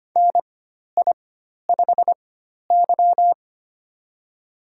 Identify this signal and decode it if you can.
Morse code, heard as a keyed tone, about 25 wpm, spaced as for 12 wpm, 690 Hz. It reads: NI5Y
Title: USA Call Signs - 4 Characters - 4x Character Spacing - 25wpm